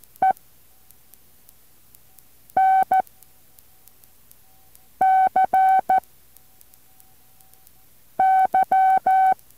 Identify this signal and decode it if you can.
Morse code, heard as a keyed tone, about 14 wpm, 760 Hz. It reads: E N C Y
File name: intl morse code2